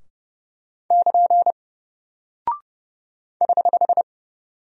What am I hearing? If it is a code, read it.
Morse code, heard as a keyed tone, about 30 wpm, 700 Hz. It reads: <KN> <HH>
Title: Single Character (Letter:Number:Punctuation:Prosign) 30wpm